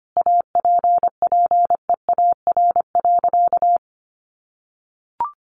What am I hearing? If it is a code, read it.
Morse code, heard as a keyed tone, about 25 wpm, 700 Hz. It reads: APPEAR.
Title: Sentences from Top 1200 Words 25wpm